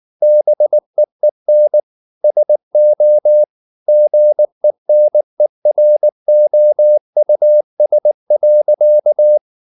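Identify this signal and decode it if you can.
Morse code, heard as a keyed tone, about 19 wpm, 600 Hz. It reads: BEEN SO GENEROUS.
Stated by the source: Morse code at 19 wpm from CW med Gnister, reading Ken Follett's novel Never in English.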